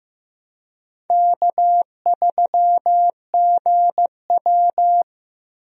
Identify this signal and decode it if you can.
Morse code, heard as a keyed tone, about 15 wpm, 700 Hz. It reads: K3GW
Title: USA Call Signs - 4 Characters - Repeated 3x - 15wpm